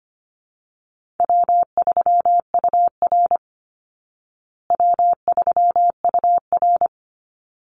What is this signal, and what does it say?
Morse code, heard as a keyed tone, about 25 wpm, 700 Hz. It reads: W3UR W3UR
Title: Easy Call Signs Repeated 2x - 25wpm